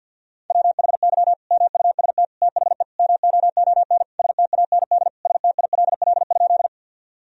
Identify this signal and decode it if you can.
Morse code, heard as a keyed tone, about 52 wpm, 700 Hz. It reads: WHY MUST THE MOON STAND STILL?
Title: Sentences from Top 1000 Words - Review Part 3 - 50wpm